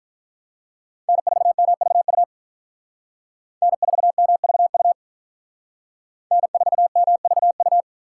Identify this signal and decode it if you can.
Morse code, heard as a keyed tone, about 43 wpm, 700 Hz. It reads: N4MVU N4MVU N4MVU
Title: Speed Racing - USA 1x3 Call Signs 35wpm